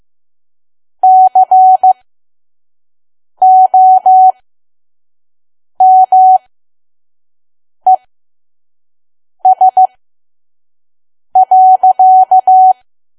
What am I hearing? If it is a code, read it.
Morse code, heard as a keyed tone, about 15 wpm, spaced as for 5 wpm, 750 Hz. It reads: COMES.
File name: Archive / 5 WPM files / PM